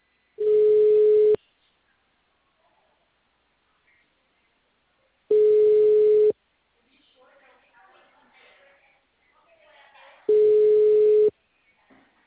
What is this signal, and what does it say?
Ringback tone (Europe, 425 Hz), with speech.